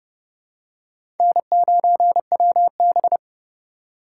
Morse code, heard as a keyed tone, about 30 wpm, 700 Hz. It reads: N9WB
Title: USA Call Signs - 4 Characters - 2x Character Spacing - 30wpm